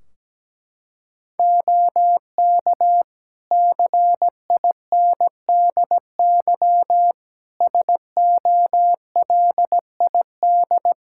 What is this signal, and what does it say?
Morse code, heard as a keyed tone, about 17 wpm, 700 Hz. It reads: OK CINDY SOLID
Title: K1USN SST QSO Practice 17wpm